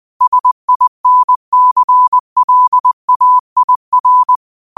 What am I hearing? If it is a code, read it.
Morse code, heard as a keyed tone, about 20 wpm, 1000 Hz. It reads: SINCLAIR